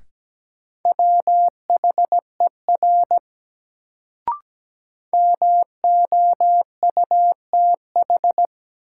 Morse code, heard as a keyed tone, about 17 wpm, 700 Hz. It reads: WHER MOUTH
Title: Sentences from Top 1500 Words - New Word Drill Alt - 17wpm